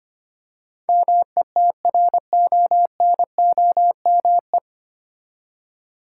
Morse code, heard as a keyed tone, about 25 wpm, 700 Hz. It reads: METRONOME
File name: practice / 25wpm / Difficult Words with Hints - 25wpm